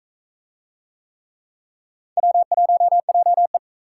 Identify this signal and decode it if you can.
Morse code, heard as a keyed tone, about 40 wpm, 700 Hz. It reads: W1JE